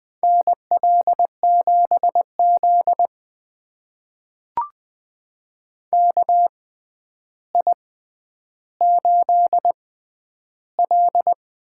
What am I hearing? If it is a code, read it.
Morse code, heard as a keyed tone, about 20 wpm, spaced as for 10 wpm, 700 Hz. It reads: NL7Z KI8L
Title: USA Call Signs - 4 Characters - 6x Character Spacing - 20wpm